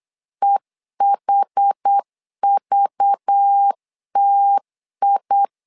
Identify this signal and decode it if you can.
Morse code, heard as a keyed tone, about 8 wpm, 800 Hz. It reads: EHVTI
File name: LEZIONI / 5 LBVC / 5 LBV 8wpm 2837